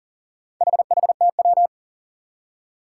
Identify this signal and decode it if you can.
Morse code, heard as a keyed tone, about 40 wpm, 700 Hz. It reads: HHTW
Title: Four Challenging Letter-Number Combinations 40wpm